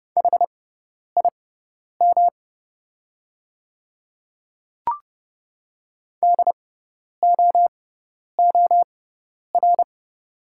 Morse code, heard as a keyed tone, about 31 wpm, spaced as for 10 wpm, 700 Hz. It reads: HIM DOOR
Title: 6x Character Spacing - Top 300 Words - 30wpm